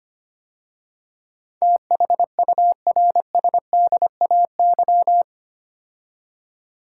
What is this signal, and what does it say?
Morse code, heard as a keyed tone, about 25 wpm, 700 Hz. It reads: THURSDAY
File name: 8 Letter Words - First 3 Letters Given - 25wpm